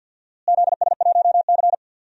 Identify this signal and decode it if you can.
Morse code, heard as a keyed tone, about 51 wpm, 700 Hz. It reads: ZS1C
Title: DXCC-100 Call Signs - 4 Characters - Repeated 3x - 50wpm